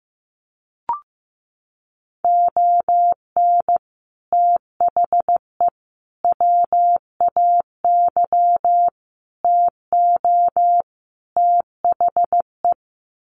Morse code, heard as a keyed tone, about 15 wpm, 700 Hz. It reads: ON THE WAY TO THE
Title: Sets of 5 Words 15wpm